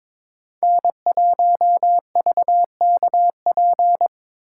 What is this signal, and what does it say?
Morse code, heard as a keyed tone, about 25 wpm, 700 Hz. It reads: N1VKP
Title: Speed Racing - USA 1x3 Call Signs 22wpm